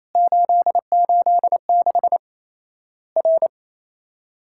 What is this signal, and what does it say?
Morse code, heard as a keyed tone, about 28 wpm, 700 Hz. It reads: 886 R
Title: Worked All Europe DX Contest - QTC Exchange - 28wpm